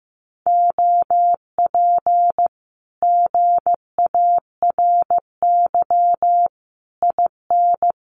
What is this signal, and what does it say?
Morse code, heard as a keyed tone, about 14 wpm, 700 Hz. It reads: OP GARY IN
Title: Common QSO Phrases Encore 15wpm